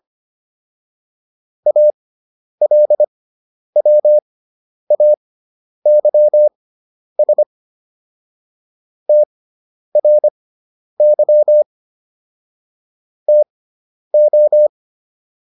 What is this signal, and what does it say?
Morse code, heard as a keyed tone, about 25 wpm, spaced as for 10 wpm, 600 Hz. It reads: ALWAYS TRY TO